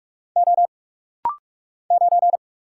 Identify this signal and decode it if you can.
Morse code, heard as a keyed tone, about 44 wpm, 700 Hz. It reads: O 9
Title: Single Letter-Number Alt - Rapid-Fire 45wpm